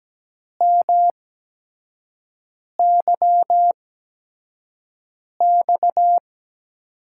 Morse code, heard as a keyed tone, about 17 wpm, spaced as for 5 wpm, 700 Hz. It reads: MYX
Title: USA Call Signs - 6 Characters - 8x Character Spacing - 17wpm